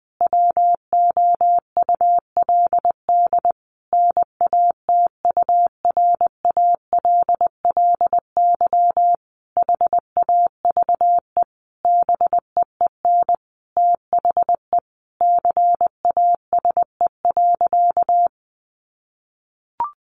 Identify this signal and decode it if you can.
Morse code, heard as a keyed tone, about 20 wpm, 700 Hz. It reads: WOULD NATURALLY HAVE BEEN THE CASE.
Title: Sentences from Top 1300 Words 20wpm